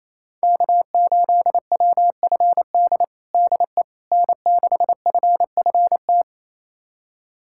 Morse code, heard as a keyed tone, about 27 wpm, 700 Hz. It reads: K8WFD DE N6FFT
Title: USA Callsign DE Callsign - 5 Characters - 28wpm